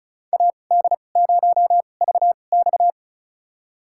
Morse code, heard as a keyed tone, about 36 wpm, spaced as for 25 wpm, 700 Hz. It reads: AD0VX
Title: USA Call Signs - 2x Character Spacing - 35wpm